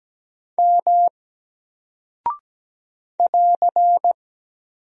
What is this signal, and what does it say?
Morse code, heard as a keyed tone, about 17 wpm, 700 Hz. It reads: M <AR>
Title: Single Character (Letter:Number:Punctuation:Prosign) 17wpm